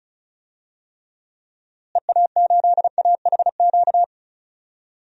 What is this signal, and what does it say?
Morse code, heard as a keyed tone, about 36 wpm, 700 Hz. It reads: EA8AHQ
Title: DXCC-100 Call Signs - 6 Characters - Repeated 3x - 35wpm